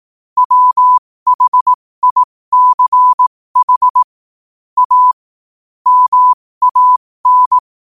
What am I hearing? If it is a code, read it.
Morse code, heard as a keyed tone, about 18 wpm, spaced as for 15 wpm, 1000 Hz. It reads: WHICH A MAN
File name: qotdfiles / 15wpm